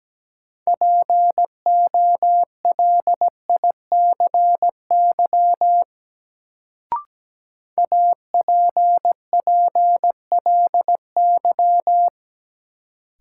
Morse code, heard as a keyed tone, about 17 wpm, 700 Hz. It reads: POLICY APPLY
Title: Sentences from Top 1200 Words - New Word Drill Alt - 17wpm